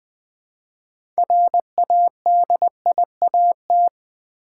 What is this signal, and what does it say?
Morse code, heard as a keyed tone, about 20 wpm, 700 Hz. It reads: RADIAT
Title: Build A Word - 3 Letters At A Time - 9-Letter Words - 20wpm